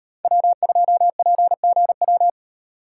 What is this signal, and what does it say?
Morse code, heard as a keyed tone, about 34 wpm, 700 Hz. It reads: W2PGW